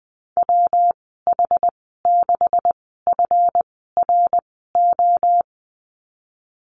Morse code, heard as a keyed tone, about 20 wpm, spaced as for 15 wpm, 700 Hz. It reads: WH6FRO
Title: 173 Lesson 28g - Number 9 - Call Signs 20wpm